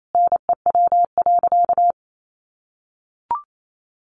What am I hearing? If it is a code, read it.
Morse code, heard as a keyed tone, about 28 wpm, 700 Hz. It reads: NEW.
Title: Sentences from Top 1300 Words 28wpm